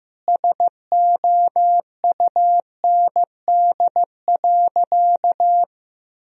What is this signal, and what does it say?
Morse code, heard as a keyed tone, about 15 wpm, 700 Hz. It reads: SOUND.
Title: Sentences from Top 1000 Words - Review Part 2 - 15wpm